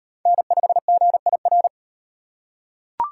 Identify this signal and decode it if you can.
Morse code, heard as a keyed tone, about 33 wpm, 700 Hz. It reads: N5GIR